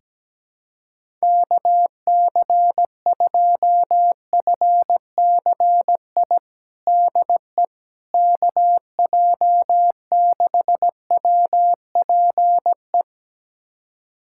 Morse code, heard as a keyed tone, about 17 wpm, 700 Hz. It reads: KC2FCI DE KJ6WPE
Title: USA Callsign DE Callsign 17wpm